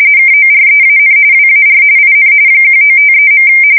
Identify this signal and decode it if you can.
RTTY, about 45.45 baud. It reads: KOREAN WAR IN 1953.
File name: fsk170 / battleshipnj